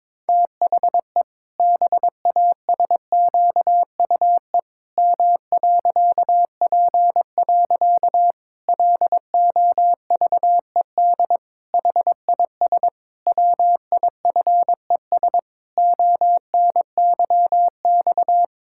Morse code, heard as a keyed tone, about 22 wpm, 700 Hz. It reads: THE BASQUE M.P. LOVED HIS WIFES ONYX
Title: Pangram Challenge v2 - 22wpm